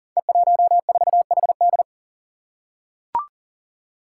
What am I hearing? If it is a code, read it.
Morse code, heard as a keyed tone, about 41 wpm, 700 Hz. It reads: E14HD
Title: Five Challenging Letter-Number Combinations 40wpm